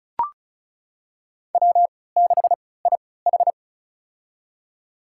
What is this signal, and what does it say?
Morse code, heard as a keyed tone, about 36 wpm, spaced as for 20 wpm, 700 Hz. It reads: W6IH
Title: USA Call Signs - 4 Characters - 3x Character Spacing - 35wpm